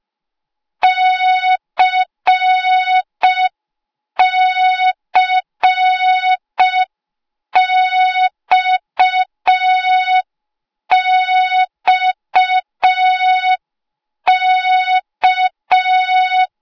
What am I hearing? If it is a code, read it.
Morse code, heard as a keyed tone, about 5 wpm, 3700 Hz. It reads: CCXXK